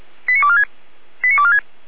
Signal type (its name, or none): five-tone selective call